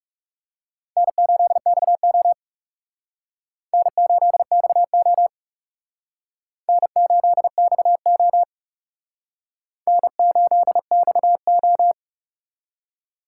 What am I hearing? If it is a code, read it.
Morse code, heard as a keyed tone, about 37 wpm, 700 Hz. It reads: N8XO N8XO N8XO N8XO
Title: Speed Racing - USA 1x2 Call Signs 30wpm